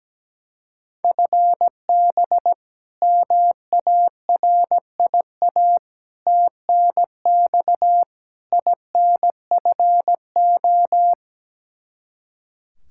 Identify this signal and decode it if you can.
Morse code, heard as a keyed tone, about 17 wpm, 700 Hz. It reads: FB MARIA TNX INFO
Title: K1USN SST QSO Practice 17wpm